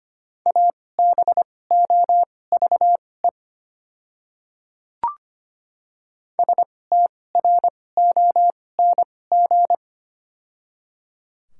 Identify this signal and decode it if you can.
Morse code, heard as a keyed tone, about 25 wpm, spaced as for 18 wpm, 700 Hz. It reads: ABOVE STRONG
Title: 2x Character Spacing - Top 500 Words - 25wpm